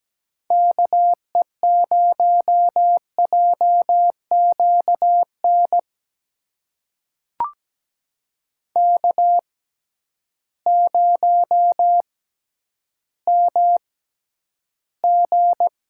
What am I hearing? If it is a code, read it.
Morse code, heard as a keyed tone, about 17 wpm, spaced as for 10 wpm, 700 Hz. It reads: KE0JQN K0MG
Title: USA Call Signs - 6x Character Spacing - 17wpm